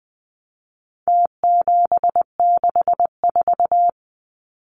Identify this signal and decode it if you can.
Morse code, heard as a keyed tone, about 20 wpm, 700 Hz. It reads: T764
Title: Numbers - Contest Style - 4-Digit 20wpm